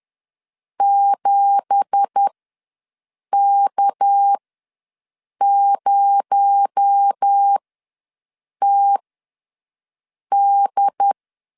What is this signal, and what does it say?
Morse code, heard as a keyed tone, about 10 wpm, spaced as for 6 wpm, 800 Hz. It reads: 7K0TD
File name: LEZIONI / 9 2867 ws 10wpm 4215